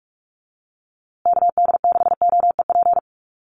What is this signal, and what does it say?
Morse code, heard as a keyed tone, about 44 wpm, 700 Hz. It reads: KD6OEP
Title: USA Call Signs Repeated 2x 45wpm